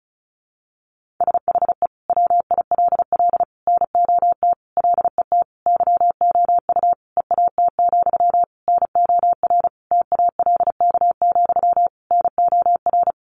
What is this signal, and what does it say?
Morse code, heard as a keyed tone, about 34 wpm, 700 Hz. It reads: SHE WILL NOT LET YOU EAT, NOR TALK, NOR
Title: Sentences from Top 1500 Words 35wpm